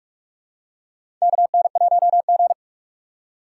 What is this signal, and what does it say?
Morse code, heard as a keyed tone, about 46 wpm, 700 Hz. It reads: KN1G
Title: USA Call Signs - 4 Characters - 4x Character Spacing - 45wpm